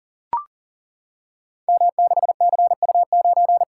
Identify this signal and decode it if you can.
Morse code, heard as a keyed tone, about 42 wpm, 700 Hz. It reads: M6CU9
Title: Code Groups - 5 Letter:Numbers - 40wpm